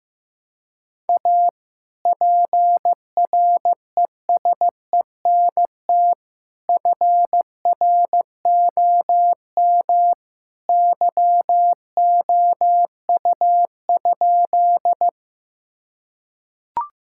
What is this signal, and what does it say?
Morse code, heard as a keyed tone, about 15 wpm, 700 Hz. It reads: A PRESENT FROM YOU?
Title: Sentences from Top 1000 Words - Review Part 5 - 15wpm